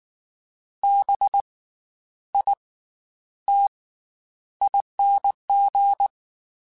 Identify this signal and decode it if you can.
Morse code, heard as a keyed tone, about 19 wpm, 800 Hz. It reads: B I T ING
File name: Special INGs / Special INGs-mp3s / Slow ing @18